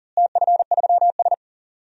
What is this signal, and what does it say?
Morse code, heard as a keyed tone, about 39 wpm, 700 Hz. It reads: TF3S
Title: DXCC-100 Call Signs - 4 Characters - Repeated 2x - 40wpm